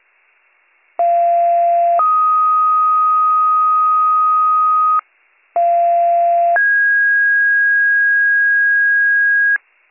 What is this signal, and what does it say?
Two-tone page, heard as a one-second tone, then a three-second tone, about 690 then 1250 Hz, about 690 then 1650 Hz, with noise. It is heard from 1.0 to 9.6 s.